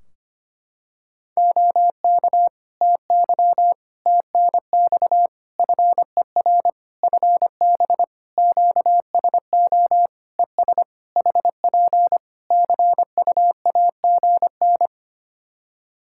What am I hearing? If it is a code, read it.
Morse code, heard as a keyed tone, about 25 wpm, 700 Hz. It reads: OK TY TNX FER FB QSO ES HP CUAGN
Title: K1USN SST QSO Practice 25wpm